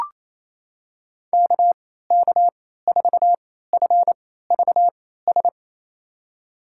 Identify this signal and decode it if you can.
Morse code, heard as a keyed tone, about 28 wpm, spaced as for 16 wpm, 700 Hz. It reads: KK4FVS
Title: USA Call Signs - 3x Character Spacing - 28wpm